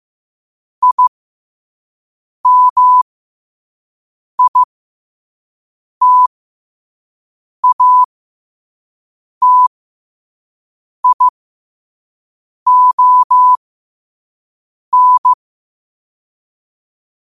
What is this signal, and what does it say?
Morse code, heard as a keyed tone, about 15 wpm, spaced as for 5 wpm, 1000 Hz. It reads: IMITATION
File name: qotdfiles / 5wpm